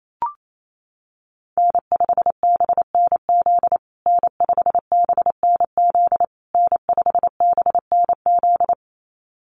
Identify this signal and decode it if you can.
Morse code, heard as a keyed tone, about 28 wpm, 700 Hz. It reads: N5BNZ N5BNZ N5BNZ
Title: USA Call Signs Repeated 3x 28wpm